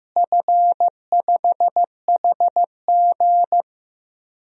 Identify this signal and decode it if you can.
Morse code, heard as a keyed tone, about 15 wpm, 700 Hz. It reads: F5HG
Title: DXCC-100 Call Signs - 4 Characters - Repeated 3x - 15wpm